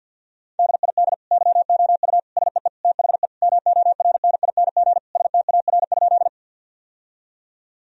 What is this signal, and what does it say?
Morse code, heard as a keyed tone, about 52 wpm, 700 Hz. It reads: DID YOU SEE THE MORNING STAR?
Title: Sentences from Top 1000 Words - Review Part 2 - 50wpm